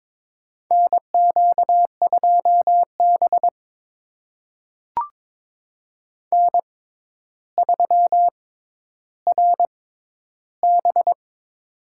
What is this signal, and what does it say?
Morse code, heard as a keyed tone, about 22 wpm, spaced as for 11 wpm, 700 Hz. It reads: NQ2B N3RB
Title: USA Call Signs - 4 Characters - 6x Character Spacing - 22wpm